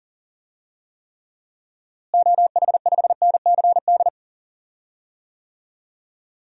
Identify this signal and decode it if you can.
Morse code, heard as a keyed tone, about 40 wpm, 700 Hz. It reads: OH5NCD
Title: DXCC-100 Call Signs - 6 Characters - Repeated 3x - 40wpm